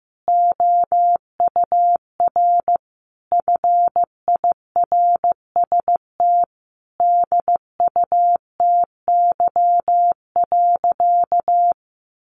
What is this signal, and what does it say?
Morse code, heard as a keyed tone, about 15 wpm, 700 Hz. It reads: OUR FIRST DUTY.
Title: Sentences from Top 1500 Words 15wpm